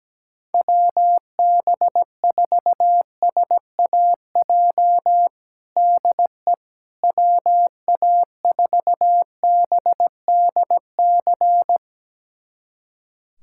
Morse code, heard as a keyed tone, about 17 wpm, 700 Hz. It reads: WB4SAJ DE WA4BDC
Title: USA Callsign DE Callsign - 6 Characters - 17wpm